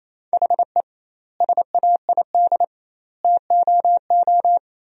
Morse code, heard as a keyed tone, about 28 wpm, 700 Hz. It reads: HE SAID TOO
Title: 2x Word-Spacing - Sentences from Top 200 Words - 28wpm